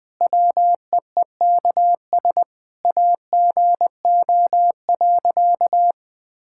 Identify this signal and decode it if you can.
Morse code, heard as a keyed tone, about 20 wpm, 700 Hz. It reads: WEEKS AGO.
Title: Sentences with 5-Word Phrases - 20wpm